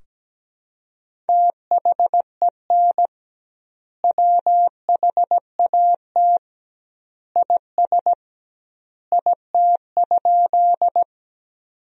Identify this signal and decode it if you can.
Morse code, heard as a keyed tone, about 17 wpm, 700 Hz. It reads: THEN WHAT IS IT?
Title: 2x Word-Spacing - Sentences from Top 100 Words - 17wpm